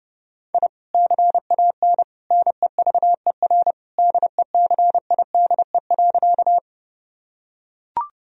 Morse code, heard as a keyed tone, about 30 wpm, 700 Hz. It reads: I CAN NEVER DECIDE.
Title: Sentences from Top 1000 Words - Review Part 3 - 30wpm